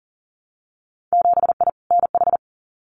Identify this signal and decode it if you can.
Morse code, heard as a keyed tone, about 39 wpm, 700 Hz. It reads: 7I NH
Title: Winter Field Day Exchanges Encore 40wpm